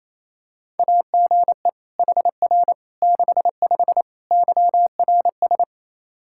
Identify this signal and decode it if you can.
Morse code, heard as a keyed tone, about 27 wpm, 700 Hz. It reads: AGE HR 65 YRS
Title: Practice QSO - Exchange 3 - More Info - 28wpm